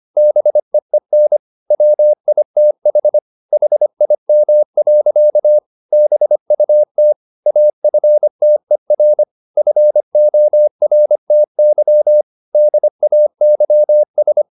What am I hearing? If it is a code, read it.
Morse code, heard as a keyed tone, about 25 wpm, 600 Hz. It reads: BEEN WITH HIM. BUT AFTER FORTY DAYS